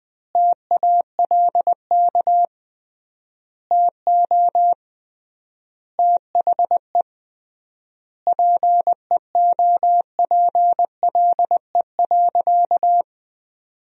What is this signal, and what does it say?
Morse code, heard as a keyed tone, about 21 wpm, 700 Hz. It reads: TALK TO THE PEOPLE.
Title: 3x Word-Spacing - Sentences from Top 400 Words - 20wpm